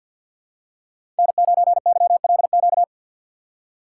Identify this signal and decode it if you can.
Morse code, heard as a keyed tone, about 52 wpm, 700 Hz. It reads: N9YLQ
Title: USA Call Signs - 5 Characters - 6x Character Spacing - 50wpm